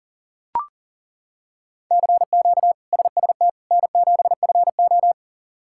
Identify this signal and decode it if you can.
Morse code, heard as a keyed tone, about 37 wpm, 700 Hz. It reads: CQ SST N7FO
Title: K1USN SST QSO Practice - 40wpm